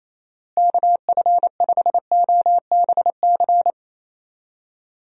Morse code, heard as a keyed tone, about 28 wpm, 700 Hz. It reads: KF5OBC
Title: USA Call Signs - 6 Characters - Repeated 2x - 28wpm